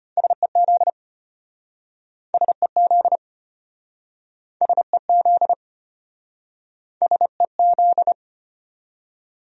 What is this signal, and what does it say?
Morse code, heard as a keyed tone, about 33 wpm, 700 Hz. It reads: SEZ SEZ SEZ SEZ